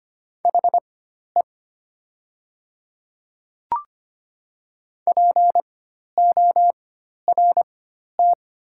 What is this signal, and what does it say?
Morse code, heard as a keyed tone, about 25 wpm, spaced as for 12 wpm, 690 Hz. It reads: HE PORT
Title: 4x Character Spacing - Top 300 Words - 25wpm